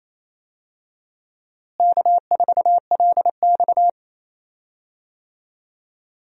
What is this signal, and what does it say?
Morse code, heard as a keyed tone, about 28 wpm, 700 Hz. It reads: K4LX
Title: USA Call Signs - 4 Characters - Repeated 3x - 28wpm